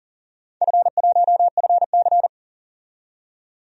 Morse code, heard as a keyed tone, about 40 wpm, 700 Hz. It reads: F1FC
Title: DXCC-100 Call Signs - 4 Characters - 40wpm